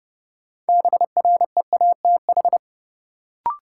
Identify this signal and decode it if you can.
Morse code, heard as a keyed tone, about 32 wpm, 700 Hz. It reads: BREATH
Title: Sets of 5 Words Encore - New Word Drill - 30wpm